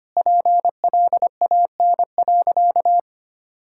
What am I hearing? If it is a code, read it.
Morse code, heard as a keyed tone, about 25 wpm, 700 Hz. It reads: PLAN.
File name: practice / 25wpm / Sentences with 4-Word Phrases - 25wpm